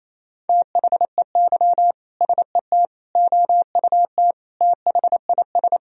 Morse code, heard as a keyed tone, about 28 wpm, 700 Hz. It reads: THEY SET OUT THIS
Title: Sentences from Top 1000 Words - Review Part 2 - 28wpm